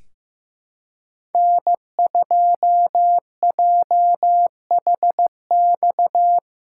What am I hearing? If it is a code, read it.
Morse code, heard as a keyed tone, about 16 wpm, 700 Hz. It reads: N2JHX
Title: USA Callsign DE Callsign - 5 Characters - 15wpm